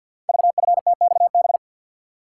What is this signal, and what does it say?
Morse code, heard as a keyed tone, about 49 wpm, 700 Hz. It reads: VFTXB
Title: Code Groups - 5 Letters - 50wpm